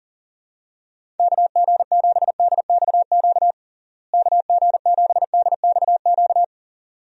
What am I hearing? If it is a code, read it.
Morse code, heard as a keyed tone, about 37 wpm, 700 Hz. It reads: KG7DXQ KG7DXQ
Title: USA Call Signs - 6 Characters - Repeated 2x - 40wpm